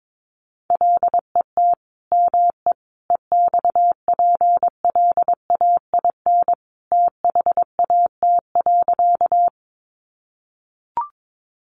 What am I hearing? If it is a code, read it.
Morse code, heard as a keyed tone, about 22 wpm, 700 Hz. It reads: LET ME EXPLAIN THAT.
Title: Sentences from Top 1100 Words 22wpm